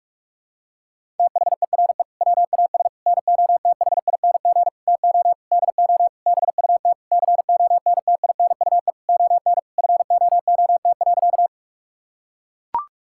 Morse code, heard as a keyed tone, about 41 wpm, 700 Hz. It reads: THERE WAS NOTHING TO DO BUT CONTINUE ON FOOT.
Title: Sentences from Top 1000 Words - Review Part 6 - 45wpm